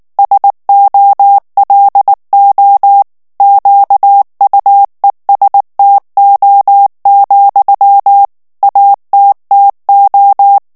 Morse code, heard as a keyed tone, about 19 wpm, 800 Hz. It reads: SOLO QUESTO, ATTO
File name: morse mp3 / ITALIEN texte clair 2 vitesse 18 mots minute